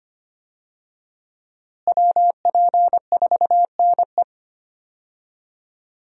Morse code, heard as a keyed tone, about 25 wpm, 700 Hz. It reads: WP4NE
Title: USA Call Signs - 5 Characters - Repeated 2x - 25wpm